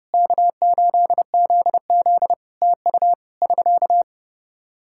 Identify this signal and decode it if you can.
Morse code, heard as a keyed tone, about 32 wpm, 700 Hz. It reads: K8ZZ TU <SK>
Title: Practice QSO - Exchange 4 - Ending - 30wpm